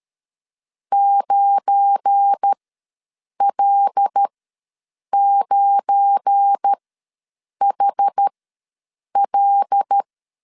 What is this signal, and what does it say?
Morse code, heard as a keyed tone, about 13 wpm, spaced as for 7 wpm, 800 Hz. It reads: 9L9HL
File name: LEZIONI / 7 GF9Z / 7 GF9 ws 12wpm 3344